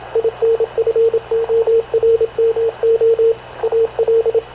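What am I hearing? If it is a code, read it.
Morse code, heard as a keyed tone, about 27 wpm, 460 Hz, with radio noise. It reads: INFORMO AL
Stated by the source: Havana Radio CLA press (PX) broadcast, recorded by Radio Officer Paul Dumesnil aboard mv CNAV QUEST/CZDO.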